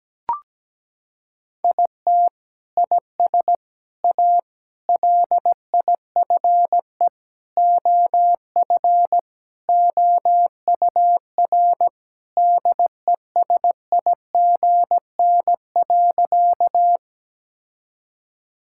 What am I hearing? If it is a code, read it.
Morse code, heard as a keyed tone, about 17 wpm, 700 Hz. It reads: IT IS A LIFE OF OUR DESIGN.
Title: Sentences from Top 1000 Words - Review Part 5 - 17wpm